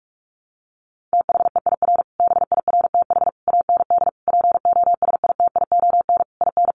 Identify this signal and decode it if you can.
Morse code, heard as a keyed tone, about 44 wpm, 700 Hz. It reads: THEIR BIRTH AND POSITION IN
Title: Sentences from Top 900 Words 45wpm